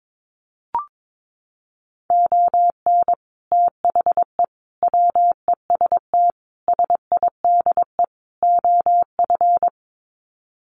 Morse code, heard as a keyed tone, about 22 wpm, 700 Hz. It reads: ON THE WEST SIDE OF
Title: Sets of 5 Words Encore2 22wpm